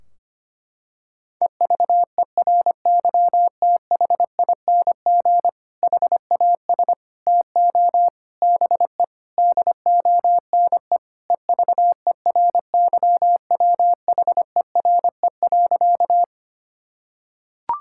Morse code, heard as a keyed tone, about 25 wpm, 700 Hz. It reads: EVERYTHING HAS TO BE DONE EVERYWHERE.
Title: Sentences from Top 1600 Words 25wpm